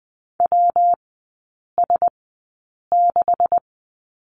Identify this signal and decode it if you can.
Morse code, heard as a keyed tone, about 19 wpm, spaced as for 8 wpm, 700 Hz. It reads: WS6
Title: 133 Lesson 22e - Question Mark - Three Characters 20wpm